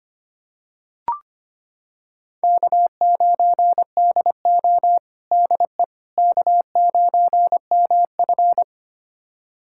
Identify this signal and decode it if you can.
Morse code, heard as a keyed tone, about 25 wpm, 700 Hz. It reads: K9DO DE K9MF
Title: USA Callsign DE Callsign - 4 Characters - 25wpm